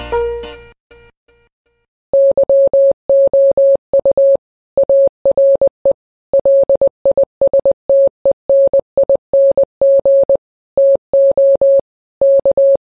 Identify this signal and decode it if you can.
Morse code, heard as a keyed tone, about 20 wpm, 550 Hz. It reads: YOU ARE LISTENING TO K
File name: radio / Imaginary Stations / KSPY 9395 kHz 29.1.23